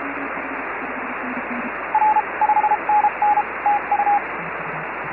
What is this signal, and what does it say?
Morse code, heard as a keyed tone, about 33 wpm, spaced as for 28 wpm, 840 Hz, with radio noise. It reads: R5NN TU
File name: FO29 JD1BLY